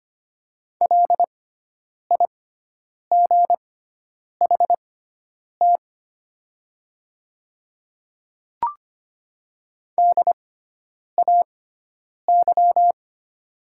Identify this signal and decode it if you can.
Morse code, heard as a keyed tone, about 25 wpm, spaced as for 9 wpm, 700 Hz. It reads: LIGHT DAY
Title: 6x Character Spacing - Top 200 Words - 25wpm